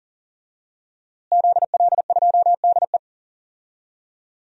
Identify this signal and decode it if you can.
Morse code, heard as a keyed tone, about 40 wpm, 700 Hz. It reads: ZL2DE
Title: DXCC-100 Call Signs - 5 Characters - Repeated 3x - 40wpm